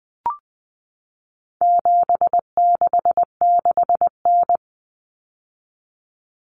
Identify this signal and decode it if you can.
Morse code, heard as a keyed tone, about 20 wpm, 700 Hz. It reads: 766N
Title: Common Cut Numbers - 4-Digit 20wpm